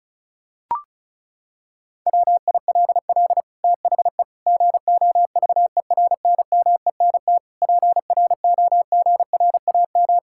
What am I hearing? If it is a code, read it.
Morse code, heard as a keyed tone, about 34 wpm, 700 Hz. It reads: WILL THE GOVERNMENT PROGRAM
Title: Sentences from Top 1000 Words - Review Part 5 - 35wpm